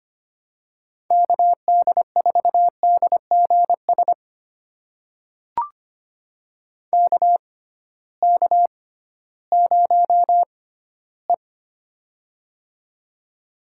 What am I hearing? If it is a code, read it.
Morse code, heard as a keyed tone, about 25 wpm, spaced as for 15 wpm, 700 Hz. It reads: KD4DGS KK0E
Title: USA Call Signs - 6x Character Spacing - 25wpm